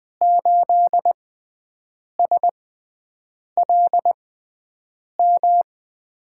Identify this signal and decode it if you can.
Morse code, heard as a keyed tone, about 21 wpm, spaced as for 7 wpm, 700 Hz. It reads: 8SLM
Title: USA Call Signs - 5 Characters - 6x Character Spacing - 20wpm